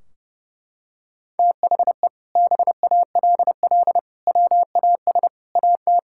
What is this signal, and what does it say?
Morse code, heard as a keyed tone, about 32 wpm, 700 Hz. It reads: THE BALL WAS AT
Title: Sentences from Top 1600 Words 30wpm